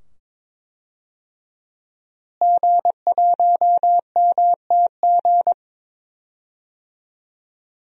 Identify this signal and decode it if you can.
Morse code, heard as a keyed tone, about 22 wpm, 700 Hz. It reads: G1MTG
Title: DXCC-100 Call Signs - 5 Characters - Repeated 2x - 22wpm